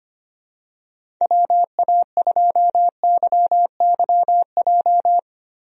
Morse code, heard as a keyed tone, about 25 wpm, 700 Hz. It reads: WA2YYJ
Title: USA Call Signs - 6 Characters - 25wpm